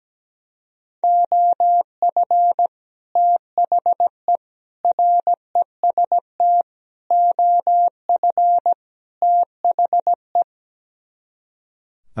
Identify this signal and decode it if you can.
Morse code, heard as a keyed tone, about 17 wpm, 700 Hz. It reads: OF THE REST OF THE
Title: Speed-Racing - Sets of 5 Words 17wpm